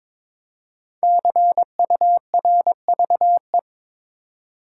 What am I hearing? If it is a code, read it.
Morse code, heard as a keyed tone, about 22 wpm, 700 Hz. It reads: CURVE
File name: 5 Letter Words - First 3 Letters Given - 22wpm